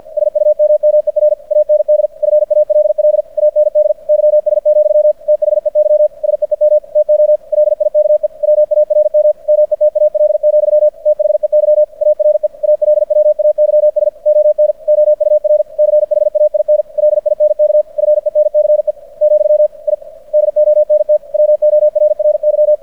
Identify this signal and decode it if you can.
Morse code, heard as a keyed tone, about 50 wpm, 600 Hz. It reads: SUMMER AND FALL AND YS, THEY SEEM TO LIKE WARM METAL, THEY ARE ALWAYS ON OUR CHAIN LINK FENCE , I DONT WORRY
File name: archived / CRSnet